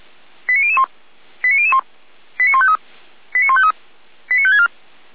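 Five-tone selective call (ZVEI), heard as quick runs of five tones, tones of about 70 ms. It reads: 89001 89001 89153 89153 89564
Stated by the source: ZVEI five-tone selective call received with an SDR.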